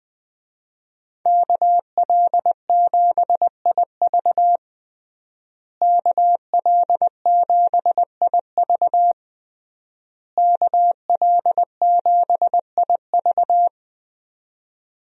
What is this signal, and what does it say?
Morse code, heard as a keyed tone, about 20 wpm, 700 Hz. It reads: KL7IV KL7IV KL7IV
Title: USA Call Signs - 5 Characters - Repeated 3x - 20wpm